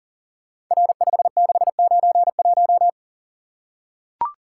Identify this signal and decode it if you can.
Morse code, heard as a keyed tone, about 38 wpm, 700 Hz. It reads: R5691
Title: Code Groups - 5 Letter:Numbers - 40wpm